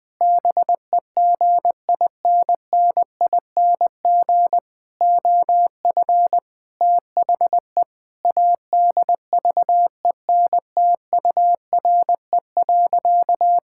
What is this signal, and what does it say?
Morse code, heard as a keyed tone, about 20 wpm, 700 Hz. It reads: BEGINNING OF THE ADVENTURE.
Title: Sentences with 4-Word Phrases - Spoken Beginning - 20wpm